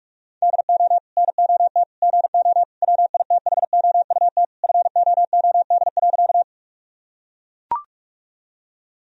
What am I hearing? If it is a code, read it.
Morse code, heard as a keyed tone, about 41 wpm, 700 Hz. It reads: DO NOT GO WITHOUT FOOD.
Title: Sentences from Top 1000 Words - Review Part 3 - 45wpm